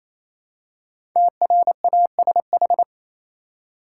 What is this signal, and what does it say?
Morse code, heard as a keyed tone, about 28 wpm, 700 Hz. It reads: TRASH
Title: Five Letter Words - 28wpm